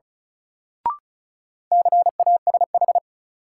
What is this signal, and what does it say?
Morse code, heard as a keyed tone, about 36 wpm, 700 Hz. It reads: CASH
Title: Sentences from Top 700 Words - New Word Drill - 35wpm